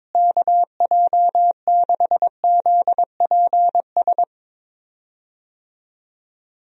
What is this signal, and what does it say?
Morse code, heard as a keyed tone, about 22 wpm, 700 Hz. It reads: KJ6ZPS
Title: USA Call Signs - 6 Characters - Repeated 2x - 22wpm